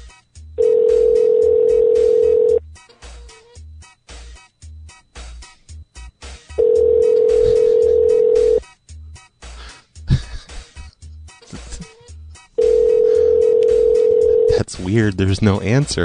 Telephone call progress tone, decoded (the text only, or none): ringback tone